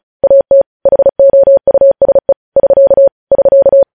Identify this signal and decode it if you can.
Morse code, heard as a keyed tone, about 36 wpm, 550 Hz. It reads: AT HOUSE <SK> <SK>
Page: News Headlines in Morse Code at 35 WPM